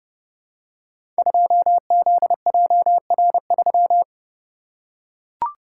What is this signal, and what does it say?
Morse code, heard as a keyed tone, about 30 wpm, 700 Hz. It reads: 2ZJR3
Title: Code Groups - 5 Letter:Numbers - 30wpm